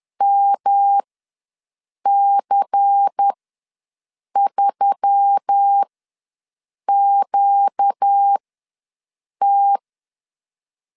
Morse code, heard as a keyed tone, about 10 wpm, spaced as for 6 wpm, 800 Hz. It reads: MC3QT
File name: LEZIONI / 6 PKX0 / 6 PKX ws 10wpm 3706